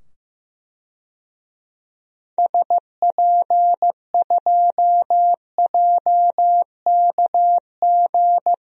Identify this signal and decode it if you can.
Morse code, heard as a keyed tone, about 15 wpm, 690 Hz. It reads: SP2JKG
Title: DXCC-100 Call Signs - 6 Characters - 15wpm